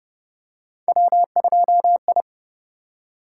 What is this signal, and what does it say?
Morse code, heard as a keyed tone, about 29 wpm, 700 Hz. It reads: W2I